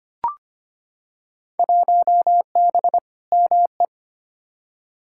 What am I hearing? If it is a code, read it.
Morse code, heard as a keyed tone, about 25 wpm, 700 Hz. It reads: 1B ME
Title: Field Day Exchanges - 25wpm